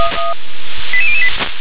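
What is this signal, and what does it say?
Five-tone selective call (ZVEI), heard as quick runs of five tones, tones of about 70 ms. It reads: 88008
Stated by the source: ZVEI five-tone selective call 88017, received on SDR.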